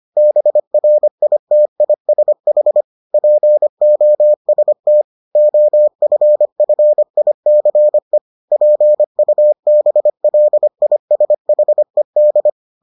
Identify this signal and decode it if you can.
Morse code, heard as a keyed tone, about 25 wpm, 600 Hz. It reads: BRITISH POST OFFICE PUBLISHED